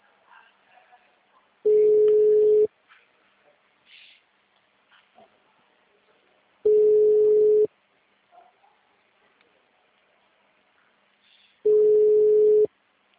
Ringback tone (Europe, 425 Hz).